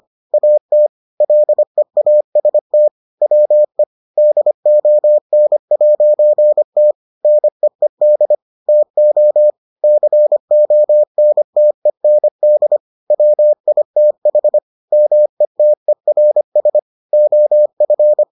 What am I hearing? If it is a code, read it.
Morse code, heard as a keyed tone, about 25 wpm, 600 Hz. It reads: AT LEAST WE DON'T NEED TO CONTEND WITH METERS OF